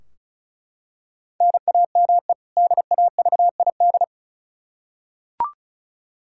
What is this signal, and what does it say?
Morse code, heard as a keyed tone, about 34 wpm, 700 Hz. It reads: NAME DAVID
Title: K1USN SST QSO Practice 35wpm